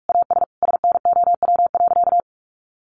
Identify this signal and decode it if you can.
Morse code, heard as a keyed tone, about 46 wpm, 700 Hz. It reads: AS SNOW.
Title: Sentences from Top 1500 Words 45wpm